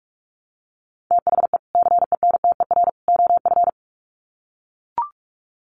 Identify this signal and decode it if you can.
Morse code, heard as a keyed tone, about 44 wpm, 700 Hz. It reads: THE CENTER OF
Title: Sets of 3 Words 45wpm